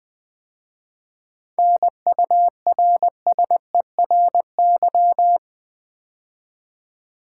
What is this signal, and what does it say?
Morse code, heard as a keyed tone, about 20 wpm, 700 Hz. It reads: NURSERY
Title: 7 Letter Words - First 3 Letters Given - 20wpm